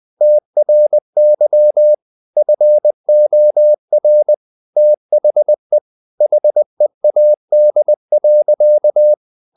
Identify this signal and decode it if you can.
Morse code, heard as a keyed tone, about 20 wpm, 600 Hz. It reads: TRY FOR THE HEAD.
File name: the old man and the sea part 2 20wpm